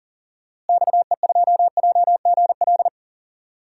Morse code, heard as a keyed tone, about 38 wpm, 700 Hz. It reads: XE2JGL
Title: DXCC-100 Call Signs - 6 Characters - Repeated 3x - 40wpm